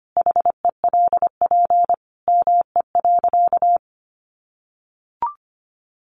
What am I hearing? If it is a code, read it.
Morse code, heard as a keyed tone, about 25 wpm, 700 Hz. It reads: HELP ME.
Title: Sentences from Top 1300 Words 25wpm